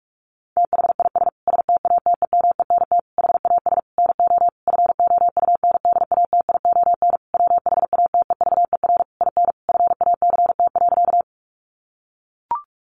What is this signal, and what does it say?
Morse code, heard as a keyed tone, about 44 wpm, 700 Hz. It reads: THIS STATEMENT HAS NO FOUNDATION WHATEVER IN FACT.
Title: Sentences from Top 1200 Words 45wpm